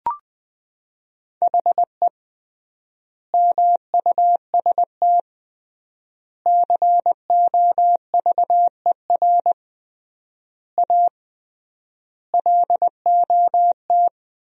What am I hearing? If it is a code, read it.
Morse code, heard as a keyed tone, about 20 wpm, 700 Hz. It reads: HE MUST COVER A LOT
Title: 3x Word-Spacing - Sentences from Top 500 Words - 20wpm